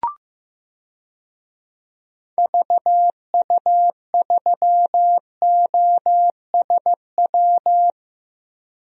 Morse code, heard as a keyed tone, about 15 wpm, 700 Hz. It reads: VU3OSW
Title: DXCC-100 Call Signs - 6 Characters - Repeated 2x - 15wpm